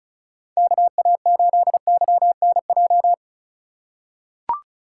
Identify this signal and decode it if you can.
Morse code, heard as a keyed tone, about 34 wpm, 700 Hz. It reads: KA8YNJ
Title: USA Call Signs - 6 Characters - 4x Character Spacing - 35wpm